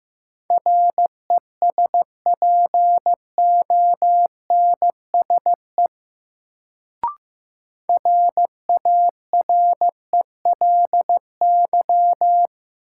Morse code, heard as a keyed tone, about 16 wpm, 700 Hz. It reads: RESPONSE RARELY
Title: Sentences from Top 1400 Words - New Word Drill Alt - 15wpm